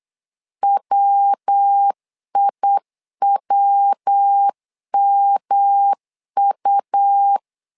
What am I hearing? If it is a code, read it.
Morse code, heard as a keyed tone, about 8 wpm, 800 Hz. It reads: WIWMU
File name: LEZIONI / 3 MWRO / 3 MW 8wpm 2501